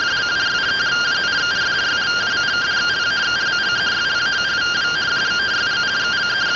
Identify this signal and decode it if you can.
Two-tone teleprinter signal, about 170 Hz shift.